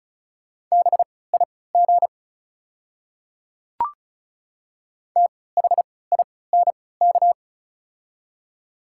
Morse code, heard as a keyed tone, about 36 wpm, spaced as for 20 wpm, 700 Hz. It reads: BIG THINK
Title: 3x Character Spacing - Top 200 Words - 35wpm